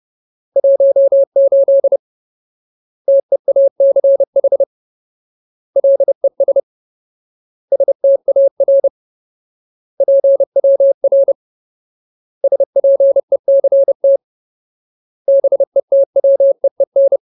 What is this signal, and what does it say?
Morse code, heard as a keyed tone, about 30 wpm, 550 Hz. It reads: 18 TEACH LES STAR PWR SPECT BETWEEN
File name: Ultimate Word List / 30WPM Ultimate